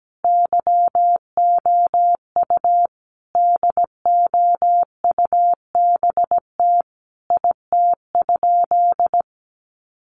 Morse code, heard as a keyed tone, about 17 wpm, 700 Hz. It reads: YOU DOUBT IT?
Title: Sentences from Top 1500 Words 17wpm